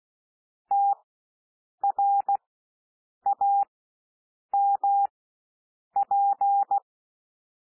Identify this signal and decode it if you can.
Morse code, heard as a keyed tone, about 16 wpm, spaced as for 7 wpm, 800 Hz. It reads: TRAMP
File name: cwmp3 / W1 SOUND WORDS 16 wpm